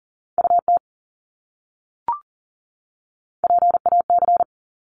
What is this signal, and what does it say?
Morse code, heard as a keyed tone, about 40 wpm, 700 Hz. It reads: UT PAC
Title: ARRL Sections 40wpm